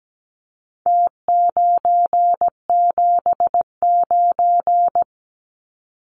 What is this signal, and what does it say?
Morse code, heard as a keyed tone, about 17 wpm, 700 Hz. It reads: T979
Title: Numbers - Contest Style - 4-Digit 17wpm